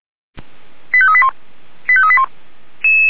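Five-tone selective call (ZVEI), heard as quick runs of five tones, tones of about 70 ms. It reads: 85381 85381
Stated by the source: ZVEI five-tone call received by SDR, feuerwehr-orlamuende.org fire brigade.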